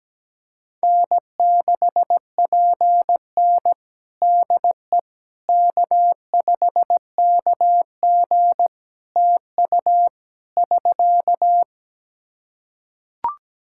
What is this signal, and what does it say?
Morse code, heard as a keyed tone, about 17 wpm, 700 Hz. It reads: N6PN DE K5KG TU <SK>
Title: Practice QSO - Exchange 4 - Ending - 17wpm